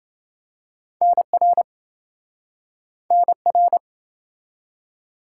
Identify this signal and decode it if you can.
Morse code, heard as a keyed tone, about 28 wpm, 700 Hz. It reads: NR NR